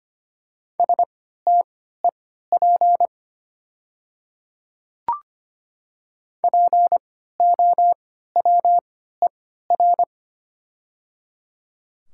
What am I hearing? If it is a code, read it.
Morse code, heard as a keyed tone, about 25 wpm, spaced as for 14 wpm, 690 Hz. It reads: STEP POWER
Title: 3x Character Spacing - Top 500 Words - 25wpm